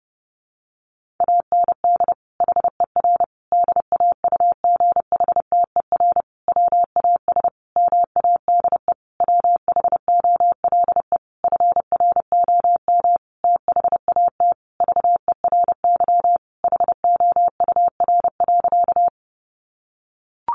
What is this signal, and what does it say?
Morse code, heard as a keyed tone, about 30 wpm, 700 Hz. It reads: AND HER DAUGHTER WAS MADE WHOLE FROM THAT VERY HOUR.
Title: Sentences from Top 1500 Words 30wpm